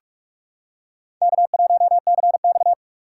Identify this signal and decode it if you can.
Morse code, heard as a keyed tone, about 41 wpm, 700 Hz. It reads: K1CX
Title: K1USN SST QSO Practice - 45wpm